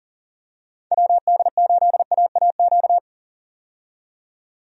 Morse code, heard as a keyed tone, about 40 wpm, 700 Hz. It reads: WD8AAQ
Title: USA Call Signs - 6 Characters - 40wpm